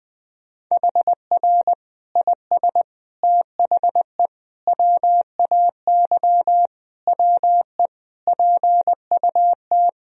Morse code, heard as a keyed tone, about 21 wpm, 700 Hz. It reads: HR IS THE WAY WE PUT
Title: Sentences from Top 200 Words and Abbreviations 20wpm